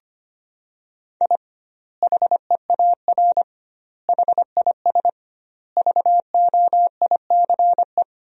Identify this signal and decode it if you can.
Morse code, heard as a keyed tone, about 25 wpm, 700 Hz. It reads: I HEAR HIS VOICE
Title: 2x Word-Spacing - Sentences from Top 500 Words - 25wpm